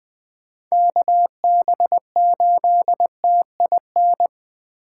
Morse code, heard as a keyed tone, about 20 wpm, 700 Hz. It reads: KB8TIN
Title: USA Call Signs - 6x Character Spacing - 20wpm